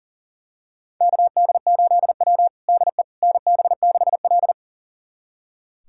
Morse code, heard as a keyed tone, about 40 wpm, 700 Hz. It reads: KD8W DE NB6L
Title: USA Callsign DE Callsign - 4 Characters - 40wpm